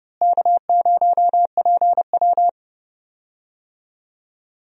Morse code, heard as a keyed tone, about 30 wpm, 700 Hz. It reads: K0PW
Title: USA Call Signs - 4 Characters - Repeated 3x - 30wpm